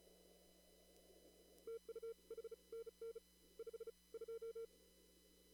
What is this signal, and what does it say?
Morse code, heard as a keyed tone, about 34 wpm, 460 Hz. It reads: TU HNN 5 2
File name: CW / kt5j wpx cw-op-k5pi